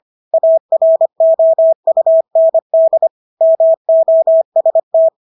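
Morse code, heard as a keyed tone, about 25 wpm, 650 Hz. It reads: AROUND MOST
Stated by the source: Morse practice of English words at 25 wpm from CW med Gnister.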